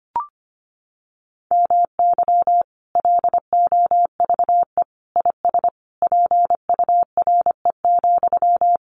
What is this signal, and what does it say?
Morse code, heard as a keyed tone, about 25 wpm, 700 Hz. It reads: MY LOVE IS PURE,
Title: Sentences from Top 1500 Words 25wpm